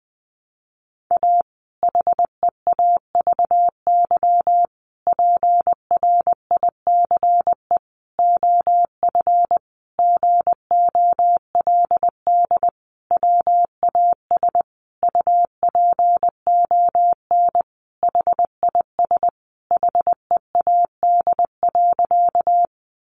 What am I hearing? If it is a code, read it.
Morse code, heard as a keyed tone, about 20 wpm, 700 Hz. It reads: A HEAVY PRICE OF GOLD WAS UPON HIS HEAD.
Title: Sentences from Top 1100 Words 20wpm